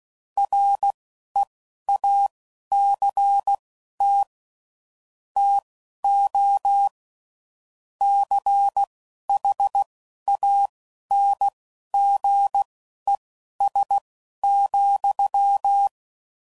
Morse code, heard as a keyed tone, about 16 wpm, spaced as for 11 wpm, 800 Hz. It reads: REACT TO CHANGES,
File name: CW Code Groups @10WPM / Technical Subjects / Reactance